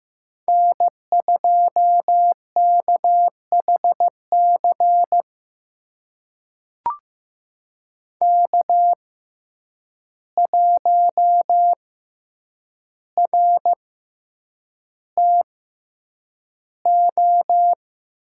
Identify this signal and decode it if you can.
Morse code, heard as a keyed tone, about 15 wpm, spaced as for 8 wpm, 700 Hz. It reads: N2KHC K1RTO
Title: USA Call Signs - 5 Characters - 6x Character Spacing - 15wpm